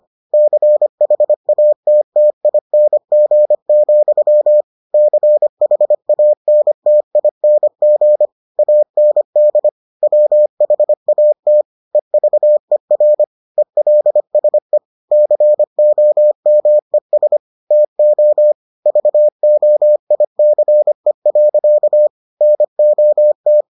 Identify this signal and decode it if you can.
Morse code, heard as a keyed tone, about 25 wpm, 600 Hz. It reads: CHATTING, CHANTING AND WHAT EVER ELSE COMES TO VOICE. NOT